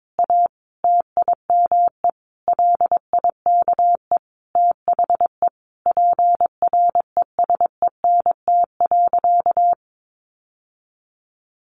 Morse code, heard as a keyed tone, about 22 wpm, 700 Hz. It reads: A TIME LIKE THE PRESENT.
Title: Sentences from Top 900 Words 22wpm